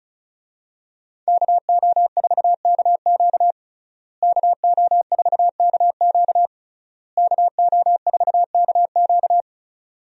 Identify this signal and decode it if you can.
Morse code, heard as a keyed tone, about 36 wpm, 700 Hz. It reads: KO4KQ KO4KQ KO4KQ
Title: USA Call Signs - 5 Characters - Repeated 3x - 35wpm